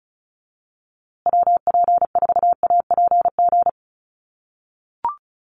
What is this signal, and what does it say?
Morse code, heard as a keyed tone, about 34 wpm, 700 Hz. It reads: WP4APG
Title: USA Call Signs Repeated 3x 35wpm